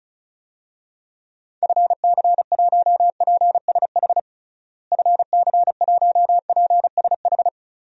Morse code, heard as a keyed tone, about 36 wpm, 700 Hz. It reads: FC1PSH FC1PSH
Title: DXCC-100 Call Signs - 6 Characters - Repeated 2x - 35wpm